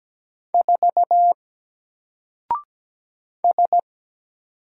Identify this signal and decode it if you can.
Morse code, heard as a keyed tone, about 17 wpm, 700 Hz. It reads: 4 S
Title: Single Character (Letter:Number:Punctuation:Prosign) Alt 17wpm